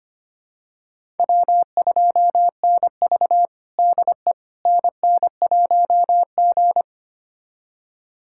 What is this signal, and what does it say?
Morse code, heard as a keyed tone, about 25 wpm, 700 Hz. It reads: W2NV DE NN1G
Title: USA Callsign DE Callsign - 4 Characters - 25wpm